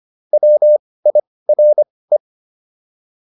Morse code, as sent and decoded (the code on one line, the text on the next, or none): .-- .. .-. .
WIRE